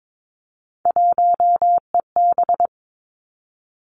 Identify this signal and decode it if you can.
Morse code, heard as a keyed tone, about 22 wpm, 700 Hz. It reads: 1EB
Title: Three Challenging Letter-Number Combinations 22wpm